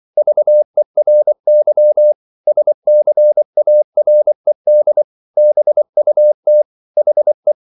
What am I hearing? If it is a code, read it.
Morse code, heard as a keyed tone, about 24 wpm, 600 Hz. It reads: VERY SCARED BUT HE